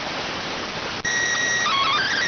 SSTV (slow-scan television), PD 290.